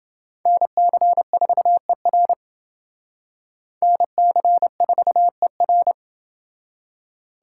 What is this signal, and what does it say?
Morse code, heard as a keyed tone, about 29 wpm, 700 Hz. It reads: NC4ER NC4ER